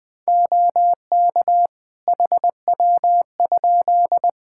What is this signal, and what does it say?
Morse code, heard as a keyed tone, about 21 wpm, 700 Hz. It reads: OK HW?
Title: K1USN SST QSO Practice 20wpm